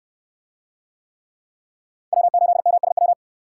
Morse code, heard as a keyed tone, about 69 wpm, 700 Hz. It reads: W8MSW